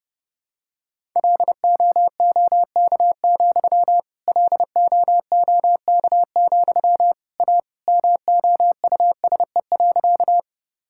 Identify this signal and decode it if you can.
Morse code, heard as a keyed tone, about 30 wpm, 700 Hz. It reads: LOOK, LOOK, A MOUSE.
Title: Sentences from Top 1000 Words - Review Part 5 - 30wpm